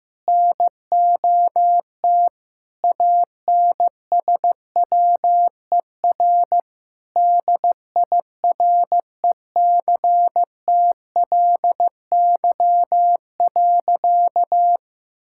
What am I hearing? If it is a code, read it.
Morse code, heard as a keyed tone, about 15 wpm, 700 Hz. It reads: NOT ANSWER DIRECTLY.
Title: Sentences from Top 1000 Words - Review Part 5 - 15wpm